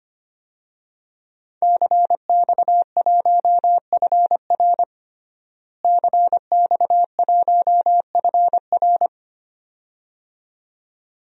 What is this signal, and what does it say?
Morse code, heard as a keyed tone, about 25 wpm, 700 Hz. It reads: CX1FR CX1FR
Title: DXCC-100 Call Signs - 5 Characters - Repeated 2x - 25wpm